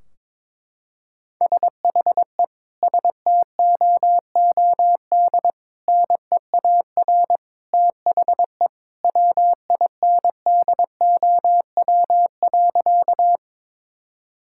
Morse code, heard as a keyed tone, about 22 wpm, 700 Hz. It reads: SHE STOOD NEAR THE WINDOW.
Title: Sentences from Top 1600 Words 22wpm